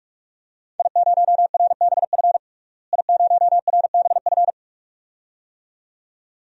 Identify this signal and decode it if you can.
Morse code, heard as a keyed tone, about 46 wpm, 700 Hz. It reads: I0RBF I0RBF
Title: DXCC-100 Call Signs - 5 Characters - Repeated 2x - 45wpm